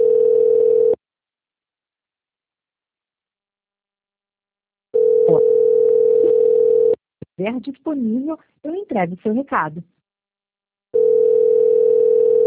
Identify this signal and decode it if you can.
Ringback tone (North America), 440 + 480 Hz, with speech.
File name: rg-1600-unknown- - - .87